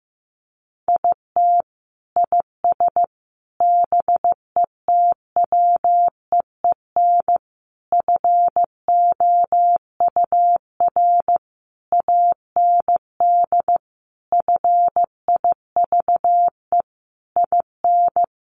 Morse code, heard as a keyed tone, about 15 wpm, 700 Hz. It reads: IT IS BETWEEN FOUR AND FIVE IN
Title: Sentences from Top 1500 Words 15wpm